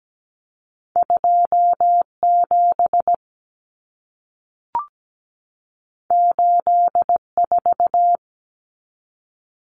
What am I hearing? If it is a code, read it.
Morse code, heard as a keyed tone, about 17 wpm, 700 Hz. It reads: 27 84
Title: Numbers - 2-Digit 17wpm